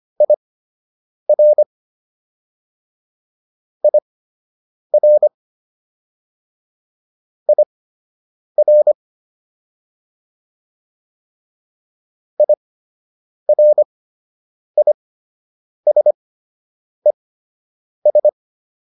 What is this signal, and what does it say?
Morse code, heard as a keyed tone, about 25 wpm, spaced as for 8 wpm, 600 Hz. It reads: IR IR IR IRISES